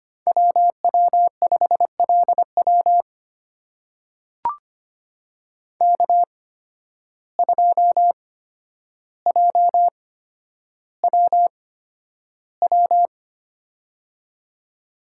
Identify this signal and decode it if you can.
Morse code, heard as a keyed tone, about 25 wpm, spaced as for 11 wpm, 700 Hz. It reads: WW5LW K2JWW
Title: USA Call Signs - 5 Characters - 8x Character Spacing - 25wpm